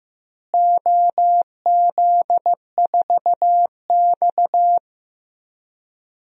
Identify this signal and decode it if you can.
Morse code, heard as a keyed tone, about 15 wpm, 700 Hz. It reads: OZ4X
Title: DXCC-100 Call Signs - 4 Characters - Repeated 3x - 15wpm